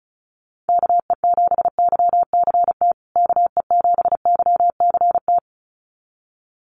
Morse code, heard as a keyed tone, about 34 wpm, 700 Hz. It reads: KE7YCT KE7YCT
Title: USA Call Signs Repeated 2x 35wpm